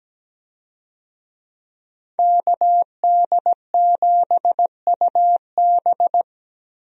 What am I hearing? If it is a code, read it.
Morse code, heard as a keyed tone, about 17 wpm, 700 Hz. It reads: KD7UB
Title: USA Call Signs - 5 Characters - Repeated 2x - 17wpm